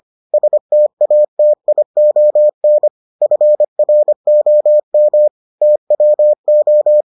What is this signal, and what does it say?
Morse code, heard as a keyed tone, about 25 wpm, 600 Hz. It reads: STATION FROM TWO